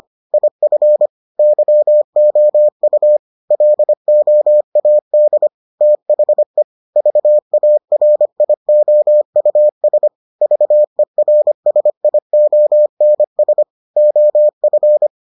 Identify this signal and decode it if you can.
Morse code, heard as a keyed tone, about 25 wpm, 600 Hz. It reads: IF YOU LOAD THE VARIOUS VERSIONS OF